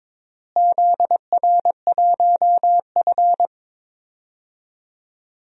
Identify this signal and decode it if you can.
Morse code, heard as a keyed tone, about 22 wpm, 700 Hz. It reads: ZR1F
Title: DXCC-100 Call Signs - 4 Characters - Repeated 2x - 22wpm